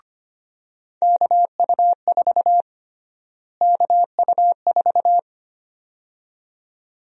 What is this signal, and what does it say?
Morse code, heard as a keyed tone, about 25 wpm, 700 Hz. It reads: KU4 KU4
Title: USA Call Sign Prefixes - Repeated 2x - 25wpm